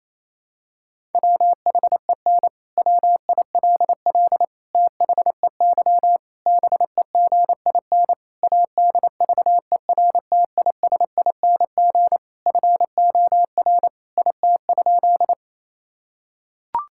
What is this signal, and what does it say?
Morse code, heard as a keyed tone, about 28 wpm, 700 Hz. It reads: WHEN WILL THEY BEGIN ADVERTISING FOR IT?
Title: Sentences from Top 1000 Words - Review Part 6 - 28wpm